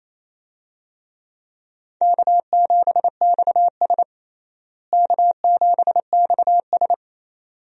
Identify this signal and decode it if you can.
Morse code, heard as a keyed tone, about 28 wpm, 700 Hz. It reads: K7XS K7XS
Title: USA Call Signs - 4 Characters - Repeated 2x - 28wpm